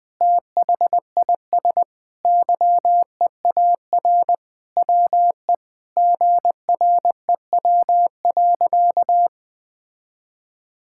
Morse code, heard as a keyed tone, about 20 wpm, 700 Hz. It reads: THIS YEAR WE GREW.
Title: Sentences with 8-Word Phrases - 20wpm